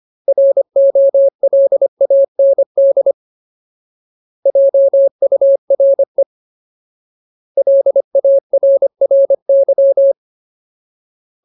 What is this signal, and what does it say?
Morse code, heard as a keyed tone, about 25 wpm, 550 Hz. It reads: ROLAND JURE LARRY
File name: Ultimate Word List / 25WPM Ultimate